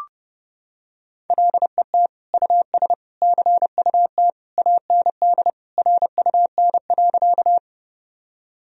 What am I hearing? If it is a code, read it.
Morse code, heard as a keyed tone, about 32 wpm, 700 Hz. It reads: LET US CUT AND RUN.
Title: Sentences from Top 1000 Words - Review Part 1 - 30wpm